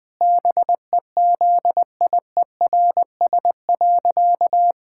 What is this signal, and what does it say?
Morse code, heard as a keyed tone, about 20 wpm, 700 Hz. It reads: BEZIERS.
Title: Pangram Challenge v2 - 20wpm